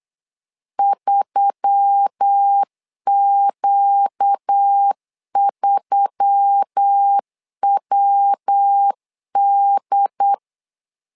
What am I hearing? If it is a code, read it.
Morse code, heard as a keyed tone, about 8 wpm, 800 Hz. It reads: 3Q3WD
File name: LEZIONI / 4 Q1S3 8wpm 3037